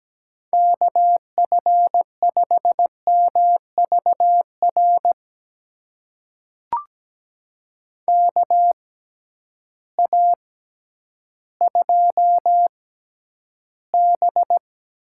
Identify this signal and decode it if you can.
Morse code, heard as a keyed tone, about 17 wpm, spaced as for 10 wpm, 700 Hz. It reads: KF5MVR KA2B
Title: USA Call Signs - 6 Characters - 6x Character Spacing - 17wpm